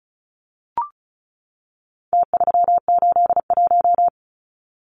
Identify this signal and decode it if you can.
Morse code, heard as a keyed tone, about 34 wpm, 700 Hz. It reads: T381
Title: Numbers - Contest Style - 4-Digit 35wpm